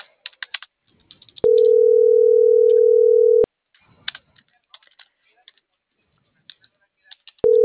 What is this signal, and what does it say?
Ringback tone (North America), 440 + 480 Hz.